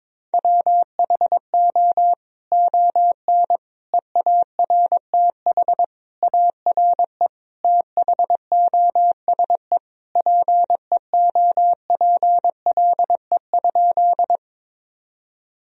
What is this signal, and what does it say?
Morse code, heard as a keyed tone, about 21 wpm, 700 Hz. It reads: WHO ON EARTH ARE THOSE PEOPLE?
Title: Sentences from Top 1000 Words - Review Part 1 - 22wpm